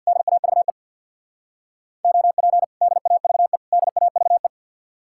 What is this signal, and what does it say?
Morse code, heard as a keyed tone, about 65 wpm, 700 Hz. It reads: DAVE OP DAVE DAVE